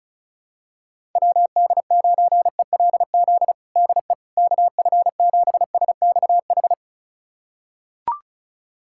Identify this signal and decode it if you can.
Morse code, heard as a keyed tone, about 36 wpm, 700 Hz. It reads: WD9ELZ DE KF7SXH
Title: USA Callsign DE Callsign - 6 Characters - 35wpm